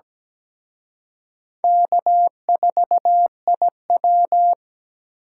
Morse code, heard as a keyed tone, about 20 wpm, 700 Hz. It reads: K4IW
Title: Speed Racing - USA 1x2 Call Signs 17wpm